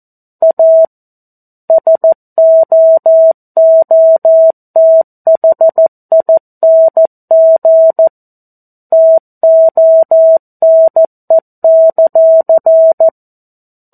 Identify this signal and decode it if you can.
Morse code, heard as a keyed tone, about 14 wpm, 650 Hz. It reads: A SOOTHING TONE;
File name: 02-Chapter-02 14 WPM